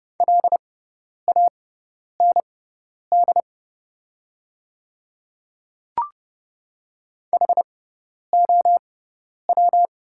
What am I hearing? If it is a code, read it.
Morse code, heard as a keyed tone, about 30 wpm, spaced as for 10 wpm, 700 Hz. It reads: LAND HOW
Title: 6x Character Spacing - Top 200 Words - 30wpm